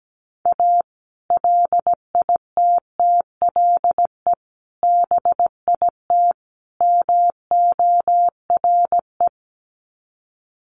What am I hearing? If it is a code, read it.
Morse code, heard as a keyed tone, about 17 wpm, 700 Hz. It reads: A LITTLE BIT MORE
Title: Sets of 4 Words 17wpm